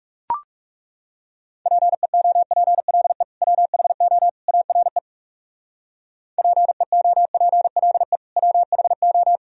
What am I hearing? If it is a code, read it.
Morse code, heard as a keyed tone, about 35 wpm, 700 Hz. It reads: PEOPLE WHO ARE PEOPLE WHO
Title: Speed-Racing - Sets of 3 Words 30wpm